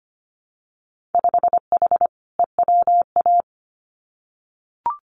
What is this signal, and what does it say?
Morse code, heard as a keyed tone, about 25 wpm, 690 Hz. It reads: 5H EWA
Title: Winter Field Day Exchanges Encore 25wpm